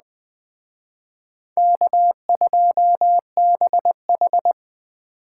Morse code, heard as a keyed tone, about 20 wpm, 700 Hz. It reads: K2BH
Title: Speed Racing - USA 1x2 Call Signs 17wpm